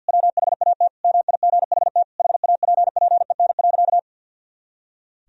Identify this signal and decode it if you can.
Morse code, heard as a keyed tone, about 50 wpm, 700 Hz. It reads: WHAT MIGHT HAPPEN.
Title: Sentences from Top 400 Words and Abbreviations 50wpm